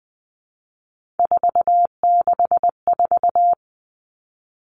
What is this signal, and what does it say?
Morse code, heard as a keyed tone, about 20 wpm, 700 Hz. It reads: 464
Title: Three Challenging Letter-Number Combinations 20wpm